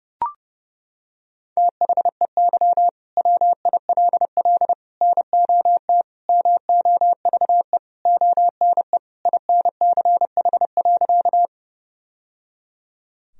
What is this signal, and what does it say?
Morse code, heard as a keyed tone, about 32 wpm, 700 Hz. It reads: THEY WILL NOT MOVE ONE INCH.
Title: Sentences from Top 1000 Words - Review Part 3 - 30wpm